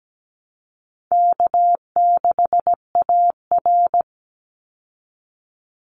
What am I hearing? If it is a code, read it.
Morse code, heard as a keyed tone, about 17 wpm, 700 Hz. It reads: K6AR
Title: Easy Call Signs Repeated 3x - 17wpm